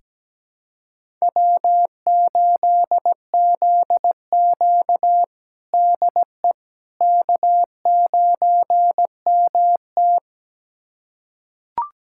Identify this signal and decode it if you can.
Morse code, heard as a keyed tone, about 17 wpm, 700 Hz. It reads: W8ZQ DE K9MT
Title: USA Callsign DE Callsign - 4 Characters - 17wpm